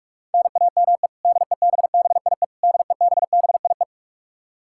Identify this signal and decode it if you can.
Morse code, heard as a keyed tone, about 41 wpm, 700 Hz. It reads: NAME DEBBIE DEBBIE
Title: Practice QSO - Exchange 1 - Beginning - 45wpm